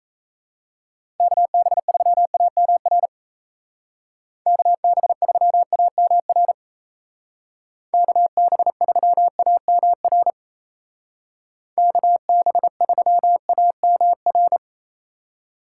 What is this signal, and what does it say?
Morse code, heard as a keyed tone, about 35 wpm, 700 Hz. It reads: KB3AMR KB3AMR KB3AMR KB3AMR
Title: Speed Racing - USA 2x3 Call Signs 28wpm